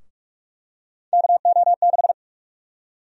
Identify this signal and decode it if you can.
Morse code, heard as a keyed tone, about 46 wpm, 700 Hz. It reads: KO6
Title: USA Call Sign Prefixes - Repeated 2x - 45wpm